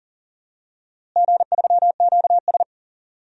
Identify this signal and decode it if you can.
Morse code, heard as a keyed tone, about 39 wpm, 700 Hz. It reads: G3QS
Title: DXCC-100 Call Signs - 4 Characters - Repeated 3x - 40wpm